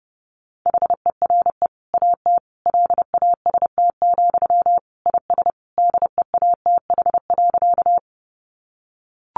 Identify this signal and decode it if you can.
Morse code, heard as a keyed tone, about 30 wpm, 700 Hz. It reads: HERE AT LAST, IS DEATH.
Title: Sentences from Top 1100 Words 30wpm